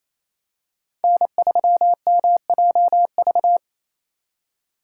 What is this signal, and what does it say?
Morse code, heard as a keyed tone, about 27 wpm, 700 Hz. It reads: N3MJV
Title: USA Call Signs - 5 Characters - 2x Character Spacing - 28wpm